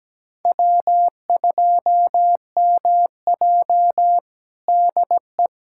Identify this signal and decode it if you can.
Morse code, heard as a keyed tone, about 17 wpm, 700 Hz. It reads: W2MJ DE
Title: USA Callsign DE Callsign - 4 Characters - 17wpm